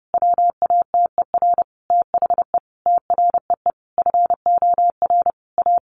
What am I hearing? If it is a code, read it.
Morse code, heard as a keyed tone, about 30 wpm, 700 Hz. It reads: WATER THE TREE FOR A
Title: Easy Phrases 30wpm